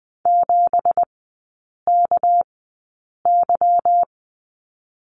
Morse code, heard as a keyed tone, about 20 wpm, spaced as for 8 wpm, 690 Hz. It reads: 7KY
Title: 198 Lesson 32e - Letter k - Three Characters 20wpm